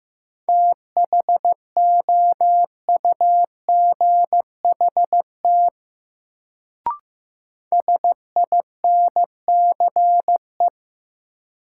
Morse code, heard as a keyed tone, about 15 wpm, 700 Hz. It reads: THOUGHT SINCE
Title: Sets of 2 Words - New Word Drill - 15wpm